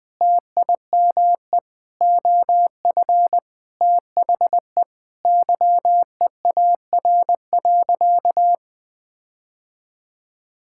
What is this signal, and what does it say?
Morse code, heard as a keyed tone, about 20 wpm, 700 Hz. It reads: TIME OF THE YEAR.
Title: Sentences with 8-Word Phrases - 20wpm